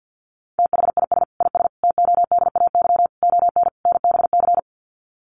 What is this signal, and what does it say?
Morse code, heard as a keyed tone, about 50 wpm, 700 Hz. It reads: THIS IS TODAY ON NBC
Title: Sets of 5 Words Encore 50wpm